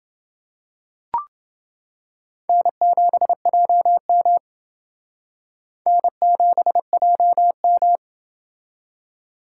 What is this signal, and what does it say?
Morse code, heard as a keyed tone, about 28 wpm, 700 Hz. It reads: N7JM N7JM